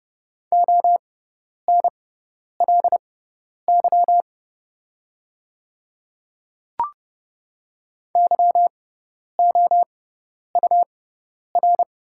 Morse code, heard as a keyed tone, about 30 wpm, spaced as for 10 wpm, 700 Hz. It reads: ONLY YOUR
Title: 6x Character Spacing - Top 300 Words - 30wpm